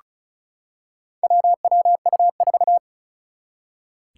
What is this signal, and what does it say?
Morse code, heard as a keyed tone, about 36 wpm, 700 Hz. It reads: WWU4
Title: Four Challenging Letter-Number Combinations 35wpm